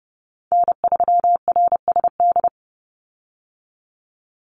Morse code, heard as a keyed tone, about 30 wpm, 700 Hz. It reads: N3RSD
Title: Call Signs Repeated 3x - 30wpm